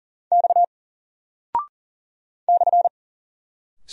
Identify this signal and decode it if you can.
Morse code, heard as a keyed tone, about 39 wpm, 700 Hz. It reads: X /
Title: Single Character (Letter:Number:Punctuation:Prosign) 40wpm